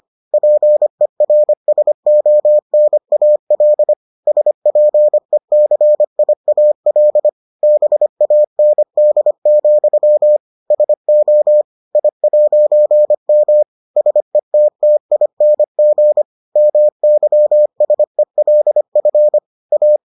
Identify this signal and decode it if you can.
Morse code, heard as a keyed tone, about 25 wpm, 600 Hz. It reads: PERSONAL SPECIAL BAND, SO I'M SETTING MYSELF A